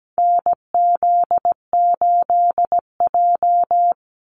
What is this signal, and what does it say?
Morse code, heard as a keyed tone, about 17 wpm, 700 Hz. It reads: NZ8J
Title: Call Signs - 17wpm